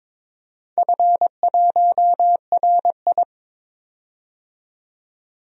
Morse code, heard as a keyed tone, about 22 wpm, 700 Hz. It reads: F1RI
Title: DXCC-100 Call Signs - 4 Characters - 22wpm